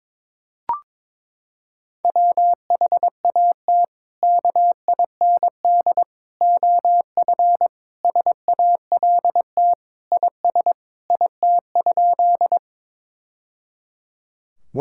Morse code, heard as a keyed tone, about 22 wpm, 700 Hz. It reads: WHAT KIND OF SALT IS IT?
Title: Sentences from Top 1000 Words - Review Part 5 - 22wpm